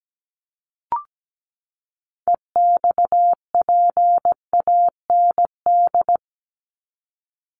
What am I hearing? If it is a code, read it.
Morse code, heard as a keyed tone, about 17 wpm, 700 Hz. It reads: EXPAND
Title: Top 1251-1500 Words 17wpm